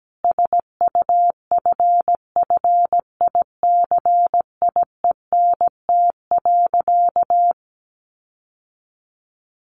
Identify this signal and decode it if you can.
Morse code, heard as a keyed tone, about 17 wpm, 700 Hz. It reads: SUFFICIENT.
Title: Sentences from Top 1400 Words 17wpm